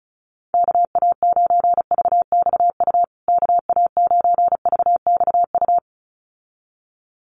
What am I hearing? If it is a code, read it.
Morse code, heard as a keyed tone, about 34 wpm, 700 Hz. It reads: KA9VXU KA9VXU
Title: USA Call Signs Repeated 2x 35wpm